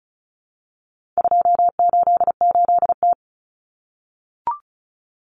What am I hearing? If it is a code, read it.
Morse code, heard as a keyed tone, about 34 wpm, 700 Hz. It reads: 288T
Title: Common Cut Numbers - 4-Digit 35wpm